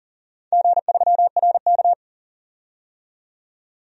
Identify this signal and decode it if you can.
Morse code, heard as a keyed tone, about 40 wpm, 700 Hz. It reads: G3RK
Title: DXCC-100 Call Signs - 4 Characters - 40wpm